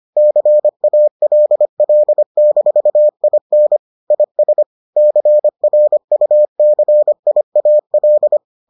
Morse code, heard as a keyed tone, about 25 wpm, 600 Hz. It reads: CALL-IN IS CRUCIAL